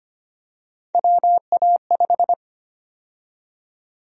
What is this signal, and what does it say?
Morse code, heard as a keyed tone, about 25 wpm, 700 Hz. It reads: WA5
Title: USA Call Sign Prefixes - Repeated 2x - 25wpm